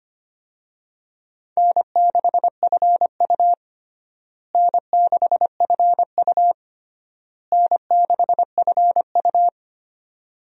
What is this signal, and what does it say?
Morse code, heard as a keyed tone, about 25 wpm, 700 Hz. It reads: N6FU N6FU N6FU
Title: USA Call Signs - 4 Characters - Repeated 3x - 25wpm